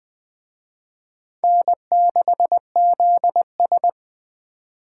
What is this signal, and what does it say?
Morse code, heard as a keyed tone, about 20 wpm, 700 Hz. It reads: N6ZS